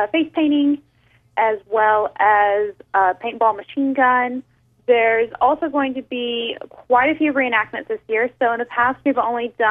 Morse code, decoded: none